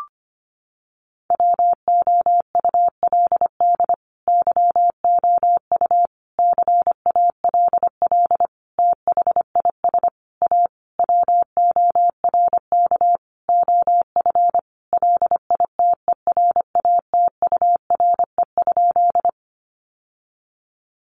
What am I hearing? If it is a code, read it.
Morse code, heard as a keyed tone, about 25 wpm, 700 Hz. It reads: WOULD YOU CALL THIS A WORK OF LITERATURE?
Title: Sentences from Top 600 Words 25wpm